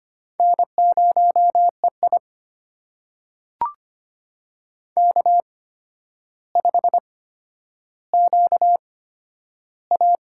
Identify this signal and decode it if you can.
Morse code, heard as a keyed tone, about 25 wpm, spaced as for 11 wpm, 700 Hz. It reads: N0EI K5QA
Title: USA Call Signs - 4 Characters - 8x Character Spacing - 25wpm